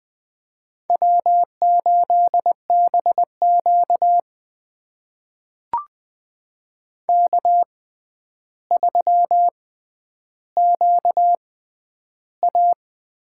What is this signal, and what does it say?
Morse code, heard as a keyed tone, about 20 wpm, spaced as for 10 wpm, 700 Hz. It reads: W8BQ K3QA
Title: USA Call Signs - 4 Characters - 6x Character Spacing - 20wpm